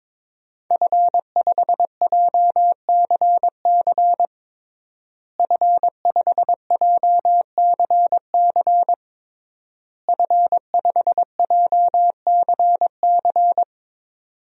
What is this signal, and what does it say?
Morse code, heard as a keyed tone, about 22 wpm, 700 Hz. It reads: F5JCC F5JCC F5JCC
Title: DXCC-100 Call Signs - 5 Characters - Repeated 3x - 22wpm